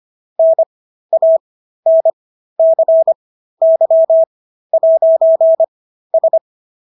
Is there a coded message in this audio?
Yes: Morse code from 0.4 to 6.4 s, heard as a keyed tone, about 25 wpm, spaced as for 13 wpm, 650 Hz. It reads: NANCY'S